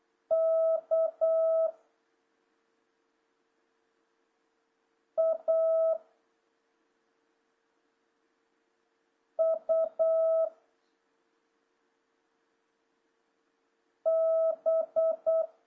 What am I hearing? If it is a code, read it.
Morse code, heard as a keyed tone, about 8 wpm, 650 Hz. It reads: K A U B
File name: jh8cbh / 2026cw